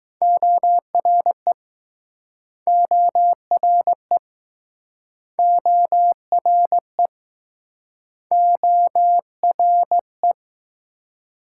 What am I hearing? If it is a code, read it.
Morse code, heard as a keyed tone, about 21 wpm, 700 Hz. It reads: ORE ORE ORE ORE